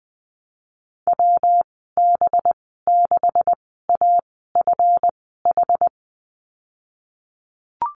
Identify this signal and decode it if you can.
Morse code, heard as a keyed tone, about 21 wpm, spaced as for 15 wpm, 700 Hz. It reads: WB6AFH
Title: 193 Lesson 31g - Letter v - Call Signs 20wpm